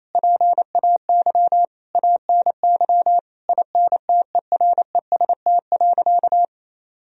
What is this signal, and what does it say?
Morse code, heard as a keyed tone, about 28 wpm, 700 Hz. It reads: PAY ANY INTEREST.
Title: Sentences from Top 1000 Words - Review Part 2 - 28wpm